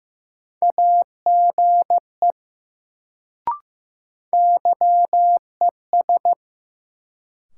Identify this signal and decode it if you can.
Morse code, heard as a keyed tone, about 15 wpm, 700 Hz. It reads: AGE YES
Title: Sentences from Top 500 Words - New Word Drill - 15wpm